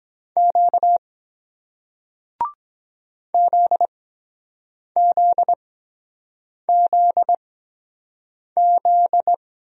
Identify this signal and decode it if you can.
Morse code, heard as a keyed tone, about 24 wpm, spaced as for 7 wpm, 700 Hz. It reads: Q ZZZZ